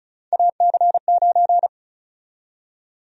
Morse code, heard as a keyed tone, about 36 wpm, 700 Hz. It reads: AC9
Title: USA Call Sign Prefixes - Repeated 2x - 35wpm